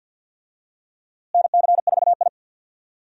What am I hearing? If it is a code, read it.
Morse code, heard as a keyed tone, about 52 wpm, 700 Hz. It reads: NC4I
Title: USA Call Signs - 4 Characters - Repeated 2x - 50wpm